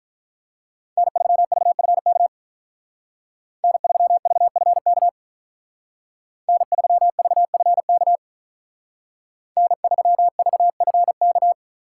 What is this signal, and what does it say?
Morse code, heard as a keyed tone, about 46 wpm, 700 Hz. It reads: N3VFK N3VFK N3VFK N3VFK